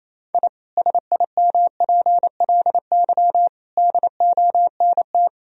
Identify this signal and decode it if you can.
Morse code, heard as a keyed tone, about 28 wpm, 700 Hz. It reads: I SIMPLY DONT
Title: Sentences from Top 1000 Words - Review Part 4 - 28wpm